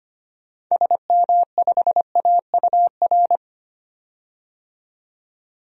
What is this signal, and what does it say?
Morse code, heard as a keyed tone, about 25 wpm, 700 Hz. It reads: SM5AUR
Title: DXCC-100 Call Signs - 6 Characters - 25wpm